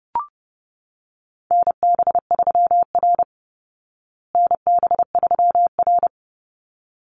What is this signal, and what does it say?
Morse code, heard as a keyed tone, about 30 wpm, 700 Hz. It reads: NB3R NB3R
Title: Call Signs Repeated 3x - 30wpm